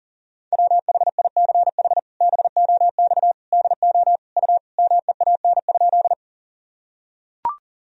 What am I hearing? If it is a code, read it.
Morse code, heard as a keyed tone, about 40 wpm, 700 Hz. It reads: WHICH BOX DO U MEAN?
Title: Sentences from Top 500 Words and Abbreviations 40wpm